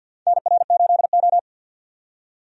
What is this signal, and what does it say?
Morse code, heard as a keyed tone, about 48 wpm, 700 Hz. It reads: NR8O
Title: USA Call Signs - 4 Characters - 50wpm